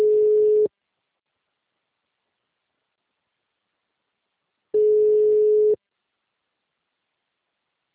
Ringback tone (Europe, 425 Hz).